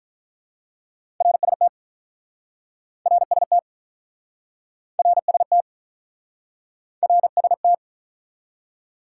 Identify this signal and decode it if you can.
Morse code, heard as a keyed tone, about 49 wpm, 700 Hz. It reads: RST RST RST RST